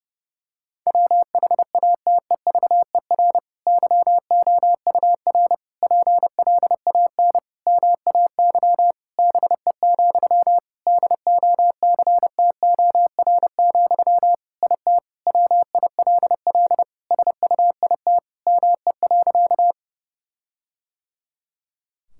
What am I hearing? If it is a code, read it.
Morse code, heard as a keyed tone, about 32 wpm, 700 Hz. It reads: WHATEVER YOUR PLAN MAY BE, DOCTOR, IT WILL SUIT ME.
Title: Sentences from Top 1600 Words 30wpm